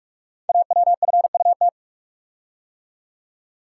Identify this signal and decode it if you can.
Morse code, heard as a keyed tone, about 44 wpm, 700 Hz. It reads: AWFUT
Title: Code Groups - 5 Letters - 45wpm